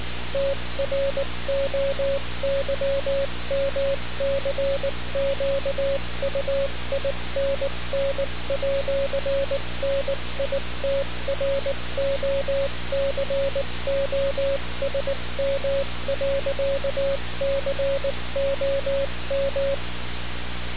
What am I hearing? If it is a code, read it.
Morse code, heard as a keyed tone, about 19 wpm, 550 Hz, with noise. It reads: TROYMCQUINN@NITROCOSM.COM
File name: radiocap / REC z